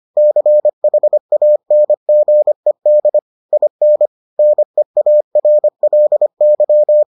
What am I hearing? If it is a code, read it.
Morse code, heard as a keyed tone, about 25 wpm, 600 Hz. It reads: CHANGED IN NEARLY